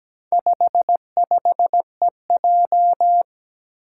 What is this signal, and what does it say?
Morse code, heard as a keyed tone, about 17 wpm, 700 Hz. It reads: 55EJ
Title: Four Challenging Letter-Number Combinations 17wpm